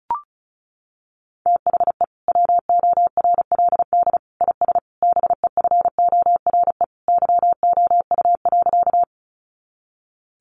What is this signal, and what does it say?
Morse code, heard as a keyed tone, about 34 wpm, 700 Hz. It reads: THE WORLD IS BEFORE YOU.
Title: Sentences from Top 200 Words 35wpm